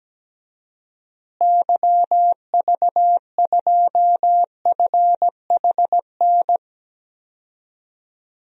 Morse code, heard as a keyed tone, about 17 wpm, 700 Hz. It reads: YV2FHN
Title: DXCC-100 Call Signs - 6 Characters - Repeated 3x - 17wpm